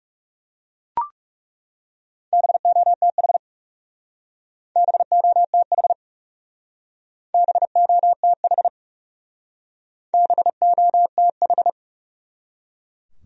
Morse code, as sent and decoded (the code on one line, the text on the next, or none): -... --- - .... / -... --- - .... / -... --- - .... / -... --- - ....
BOTH BOTH BOTH BOTH